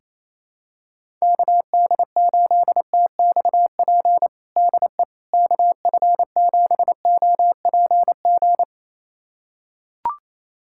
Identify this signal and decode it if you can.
Morse code, heard as a keyed tone, about 28 wpm, 700 Hz. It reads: KD8TXP DE KF7OPG
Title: USA Callsign DE Callsign - 6 Characters - 28wpm